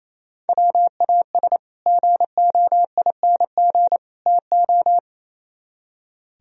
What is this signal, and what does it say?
Morse code, heard as a keyed tone, about 36 wpm, 700 Hz. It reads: WAS GOING TO